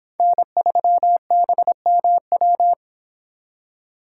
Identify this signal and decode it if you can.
Morse code, heard as a keyed tone, about 26 wpm, 700 Hz. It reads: N3BMW